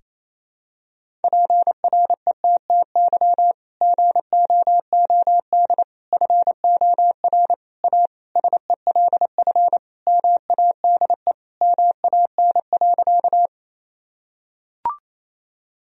Morse code, heard as a keyed tone, about 27 wpm, 700 Hz. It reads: PRETTY GOOD FOR A SELF MADE MAN.
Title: Sentences from Top 1000 Words - Review Part 1 - 28wpm